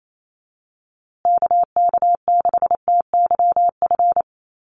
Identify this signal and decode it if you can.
Morse code, heard as a keyed tone, about 28 wpm, 700 Hz. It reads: KK6TYF
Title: USA Call Signs Repeated 2x 28wpm